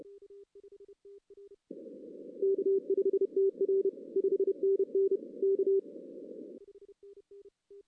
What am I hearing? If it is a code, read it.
Morse code, heard as a keyed tone, about 28 wpm, 380 Hz. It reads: K5TR 5NN K
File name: 2016 arrl dx cw / 2016-ARRL-DX-CW-K5TR- - 0-3